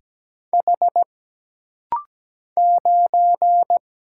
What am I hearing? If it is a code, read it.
Morse code, heard as a keyed tone, about 17 wpm, 700 Hz. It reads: H 9
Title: Single Letter-Number - Rapid-Fire 17wpm